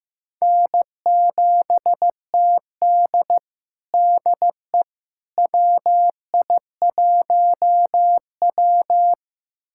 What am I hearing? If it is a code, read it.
Morse code, heard as a keyed tone, about 16 wpm, 700 Hz. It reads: N7TD DE WI1W
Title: USA Callsign DE Callsign - 4 Characters - 15wpm